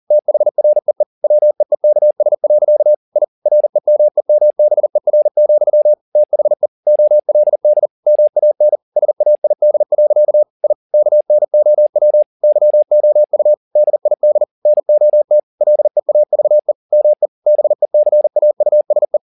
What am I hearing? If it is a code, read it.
Morse code, heard as a keyed tone, about 40 wpm, 600 Hz. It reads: THREE WEEKS. I REMEMBER, THE OLD MAN SAID. I KNOW YOU DID NOT LEAVE ME BECAUSE